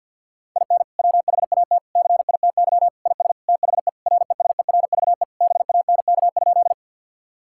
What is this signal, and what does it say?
Morse code, heard as a keyed tone, about 50 wpm, 700 Hz. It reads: IN WHAT CITY IS THE RESERVE BANK?
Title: Sentences from Top 1000 Words - Review Part 6 - 50wpm